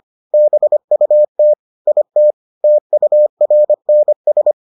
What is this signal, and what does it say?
Morse code, heard as a keyed tone, about 25 wpm, 600 Hz. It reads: BUT IT TURNS